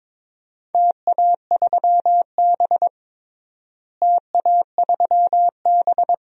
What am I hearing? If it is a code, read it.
Morse code, heard as a keyed tone, about 22 wpm, 700 Hz. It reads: TA3B TA3B
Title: DXCC-100 Call Signs - 4 Characters - Repeated 2x - 22wpm